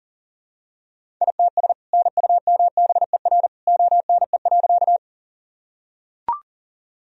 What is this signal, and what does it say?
Morse code, heard as a keyed tone, about 40 wpm, 700 Hz. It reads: ITS NUMBER ONE.
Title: Sentences from Top 1000 Words - Review Part 1 - 40wpm